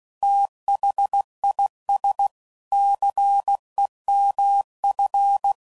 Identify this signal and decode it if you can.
Morse code, heard as a keyed tone, about 16 wpm, 800 Hz. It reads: THIS CEMF
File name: CW Code Groups @15WPM/Technical Subjects